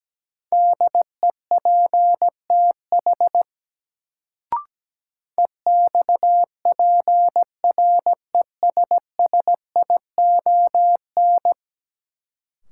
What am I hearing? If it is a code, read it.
Morse code, heard as a keyed tone, about 17 wpm, 700 Hz. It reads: DEPTH EXPRESSION
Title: Sentences from Top 1200 Words - New Word Drill Alt - 17wpm